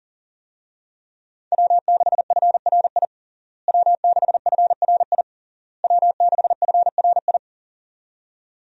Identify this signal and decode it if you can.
Morse code, heard as a keyed tone, about 40 wpm, 700 Hz. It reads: W6FRI W6FRI W6FRI
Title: USA Call Signs - 5 Characters - Repeated 3x - 40wpm